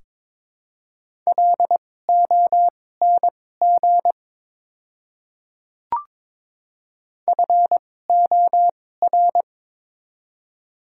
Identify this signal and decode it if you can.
Morse code, heard as a keyed tone, about 22 wpm, spaced as for 16 wpm, 700 Hz. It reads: LONG FOR
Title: 2x Character Spacing - Top 200 Words - 22wpm